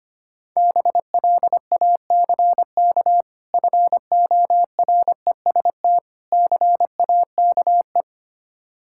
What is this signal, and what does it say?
Morse code, heard as a keyed tone, about 25 wpm, 700 Hz. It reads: BLACK FOREST CAKE
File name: practice / 25wpm / Difficult Words with Hints - 25wpm